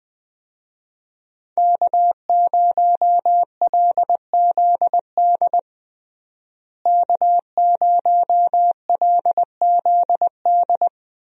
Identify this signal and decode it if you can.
Morse code, heard as a keyed tone, about 20 wpm, 700 Hz. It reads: K0LZD K0LZD
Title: USA Call Signs - 5 Characters - Repeated 3x - 20wpm